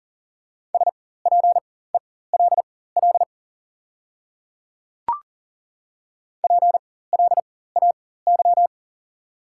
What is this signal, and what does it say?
Morse code, heard as a keyed tone, about 40 wpm, spaced as for 19 wpm, 700 Hz. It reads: SPELL PLAY
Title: 4x Character Spacing - Top 300 Words - 40wpm